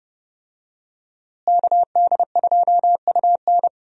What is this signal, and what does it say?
Morse code, heard as a keyed tone, about 31 wpm, 700 Hz. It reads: KD2UN
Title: USA Call Signs - 5 Characters - Repeated 3x - 30wpm